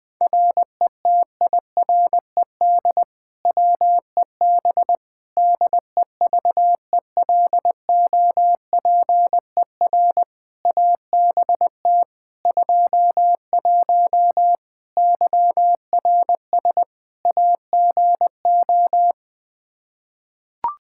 Morse code, heard as a keyed tone, about 20 wpm, 700 Hz. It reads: RETIRED WEB DEVELOPER ABT 21 YRS AGO
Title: Practice QSO - Exchange 3 - More Info - 20wpm